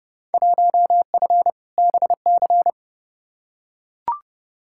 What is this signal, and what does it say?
Morse code, heard as a keyed tone, about 32 wpm, 700 Hz. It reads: 1F BC
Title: Field Day Exchanges Encore - 30wpm